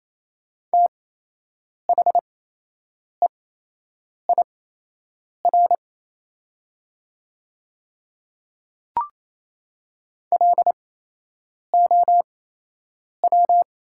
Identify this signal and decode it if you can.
Morse code, heard as a keyed tone, about 28 wpm, spaced as for 8 wpm, 700 Hz. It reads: THEIR LOW
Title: 8x Character Spacing - Top 200 Words - 28wpm